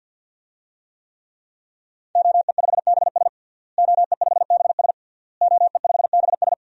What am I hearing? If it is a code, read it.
Morse code, heard as a keyed tone, about 46 wpm, 700 Hz. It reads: OE5BS OE5BS OE5BS
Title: DXCC-100 Call Signs - 5 Characters - Repeated 3x - 50wpm